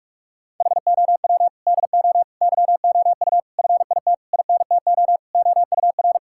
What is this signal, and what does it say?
Morse code, heard as a keyed tone, about 46 wpm, 700 Hz. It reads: HOW DO YOU FIT INTO OUR
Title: Sentences from Top 1000 Words - Review Part 6 - 45wpm